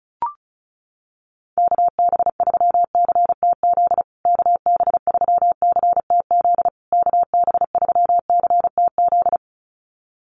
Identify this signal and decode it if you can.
Morse code, heard as a keyed tone, about 34 wpm, 700 Hz. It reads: KB3CTZ KB3CTZ KB3CTZ
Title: USA Call Signs Repeated 3x 35wpm